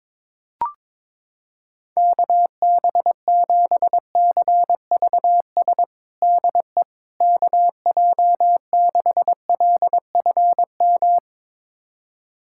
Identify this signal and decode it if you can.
Morse code, heard as a keyed tone, about 22 wpm, 700 Hz. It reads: KB7CVS DE KJ6LFM
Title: USA Callsign DE Callsign - 6 Characters - 22wpm